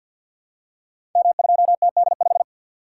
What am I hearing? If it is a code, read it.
Morse code, heard as a keyed tone, about 52 wpm, 700 Hz. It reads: M2TD5
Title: Code Groups - 5 Letter:Numbers - 50wpm